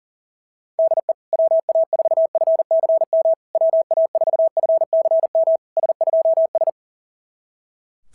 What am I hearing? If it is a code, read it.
Morse code, heard as a keyed tone, about 38 wpm, 690 Hz. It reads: DE WA4FCM WA4FCM S2S
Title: SOTA QSO Practice 40wpm